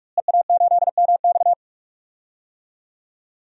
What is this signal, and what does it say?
Morse code, heard as a keyed tone, about 46 wpm, 700 Hz. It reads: EA8MX
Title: DXCC-100 Call Signs - 5 Characters - Repeated 3x - 45wpm